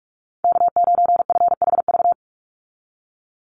Morse code, heard as a keyed tone, about 44 wpm, 700 Hz. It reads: K9FHV
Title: USA Call Signs 45wpm